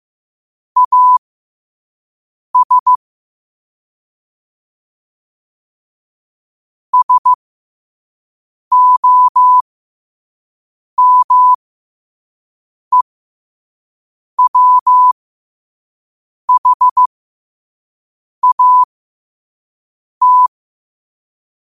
Morse code, heard as a keyed tone, about 15 wpm, spaced as for 5 wpm, 1000 Hz. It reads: AS SOMEWHAT
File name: qotdfiles / 5wpm